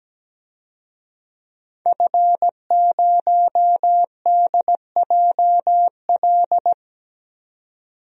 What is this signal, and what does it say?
Morse code, heard as a keyed tone, about 17 wpm, 700 Hz. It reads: F0DJL
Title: DXCC-100 Call Signs - 5 Characters - 17wpm